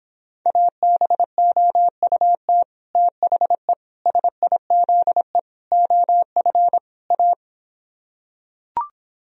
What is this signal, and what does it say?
Morse code, heard as a keyed tone, about 26 wpm, 700 Hz. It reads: ABOUT THE SIZE OF A